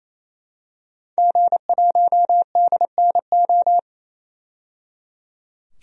Morse code, heard as a keyed tone, about 28 wpm, 700 Hz. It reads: G1DNO
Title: DXCC-100 Call Signs - 5 Characters - 28wpm